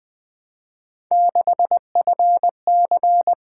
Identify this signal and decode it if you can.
Morse code, heard as a keyed tone, about 20 wpm, 700 Hz. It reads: 6FC
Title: Three Letter-Number Combinations - 20wpm